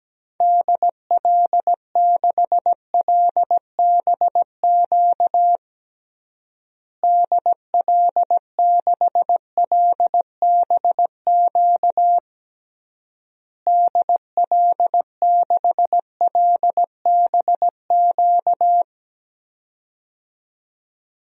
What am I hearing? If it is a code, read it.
Morse code, heard as a keyed tone, about 17 wpm, 700 Hz. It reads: DL6LBQ DL6LBQ DL6LBQ
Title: DXCC-100 Call Signs - 6 Characters - Repeated 3x - 17wpm